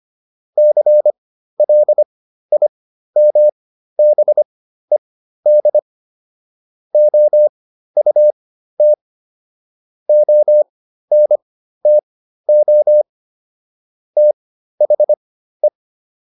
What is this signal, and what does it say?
Morse code, heard as a keyed tone, about 25 wpm, spaced as for 13 wpm, 600 Hz. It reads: CLIMBED OUT ONTO THE